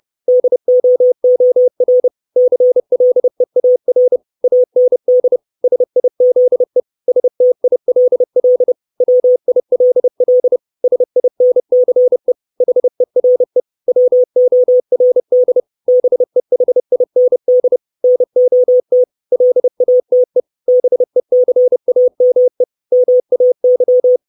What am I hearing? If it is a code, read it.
Morse code, heard as a keyed tone, about 30 wpm, 500 Hz. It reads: DOOR CLEAR AND SIZE STILL WILL SINCE HERE WORD BEHIND NOT LATE BECAME MAY
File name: 632 Words 30WPM